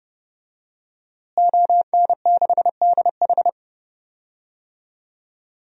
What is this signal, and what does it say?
Morse code, heard as a keyed tone, about 30 wpm, 700 Hz. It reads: ON6DH
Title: DXCC-100 Call Signs - 5 Characters - Repeated 3x - 30wpm